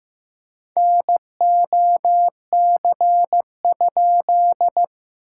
Morse code, heard as a keyed tone, about 15 wpm, 700 Hz. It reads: NOC?
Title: SOTA QSO Practice 15wpm